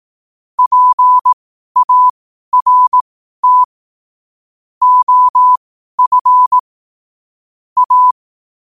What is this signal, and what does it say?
Morse code, heard as a keyed tone, about 18 wpm, spaced as for 12 wpm, 1000 Hz. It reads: PART OF A
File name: qotdfiles / 12wpm